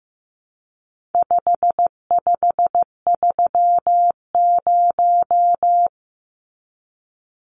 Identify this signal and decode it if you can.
Morse code, heard as a keyed tone, about 15 wpm, 700 Hz. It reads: 5530
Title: Numbers - 4-Digit 15wpm